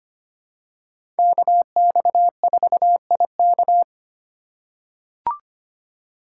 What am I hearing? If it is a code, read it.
Morse code, heard as a keyed tone, about 25 wpm, 700 Hz. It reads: KX4IK
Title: USA Call Signs - 5 Characters - 2x Character Spacing - 25wpm